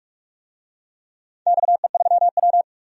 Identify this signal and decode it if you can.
Morse code, heard as a keyed tone, about 42 wpm, 700 Hz. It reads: XE3W
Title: DXCC-100 Call Signs - 4 Characters - 45wpm